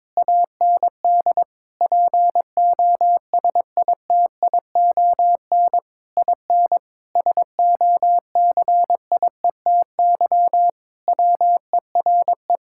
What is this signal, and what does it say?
Morse code, heard as a keyed tone, about 22 wpm, 700 Hz. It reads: AND POSITION IN SOCIETY WERE
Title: Sentences from Top 1000 Words - Review Part 5 - 22wpm